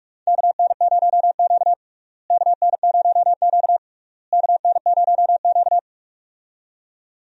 Morse code, heard as a keyed tone, about 41 wpm, 700 Hz. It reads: KN0Q KN0Q KN0Q
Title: USA Call Signs - 4 Characters - Repeated 3x - 45wpm